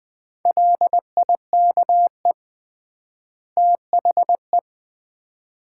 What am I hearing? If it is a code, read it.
Morse code, heard as a keyed tone, about 20 wpm, 700 Hz. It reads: LIKE THE
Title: 3x Word-Spacing - Sets of 2 Words - 20wpm